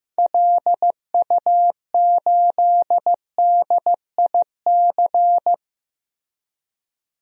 Morse code, heard as a keyed tone, about 15 wpm, 700 Hz. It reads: LU8DIC
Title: DXCC-100 Call Signs - 6 Characters - Repeated 2x - 15wpm